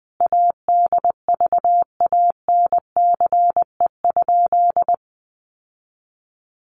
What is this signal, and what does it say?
Morse code, heard as a keyed tone, about 20 wpm, 700 Hz. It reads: ADVANCE?
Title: Sentences from Top 1400 Words 20wpm